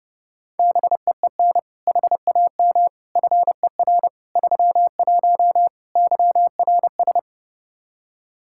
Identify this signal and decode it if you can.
Morse code, heard as a keyed tone, about 28 wpm, 700 Hz. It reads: BEEN HAM FER 31 YRS
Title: K1USN SST QSO Practice 30wpm